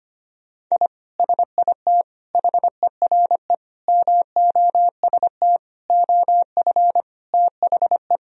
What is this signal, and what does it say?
Morse code, heard as a keyed tone, about 25 wpm, 700 Hz. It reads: I SIT HERE MOST OF THE
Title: Sentences from Top 1000 Words - Review Part 1 - 25wpm